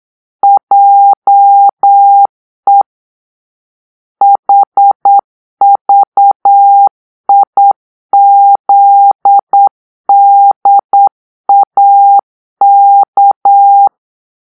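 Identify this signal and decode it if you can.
Morse code, heard as a keyed tone, about 9 wpm, 800 Hz. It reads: JE HVIZDAK
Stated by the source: Morse-code cipher made by a player team for the Kachna puzzle game.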